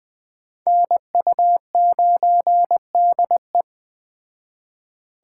Morse code, heard as a keyed tone, about 21 wpm, 700 Hz. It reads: NU9DE